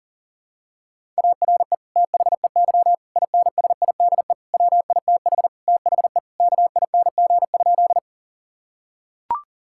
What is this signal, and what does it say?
Morse code, heard as a keyed tone, about 40 wpm, 700 Hz. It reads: ARE THEY INSIDE WITH THE KING?
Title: Sentences from Top 1000 Words - Review Part 6 - 40wpm